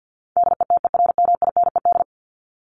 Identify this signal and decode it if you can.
Morse code, heard as a keyed tone, about 50 wpm, 700 Hz. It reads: DETERMINED
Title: ED Suffix - Top 100 Words with Base Words 50wpm